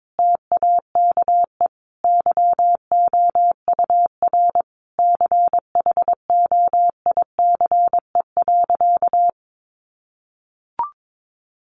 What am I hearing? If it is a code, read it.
Morse code, heard as a keyed tone, about 22 wpm, 700 Hz. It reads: TAKE YOUR CHOICE.
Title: Sentences from Top 900 Words 22wpm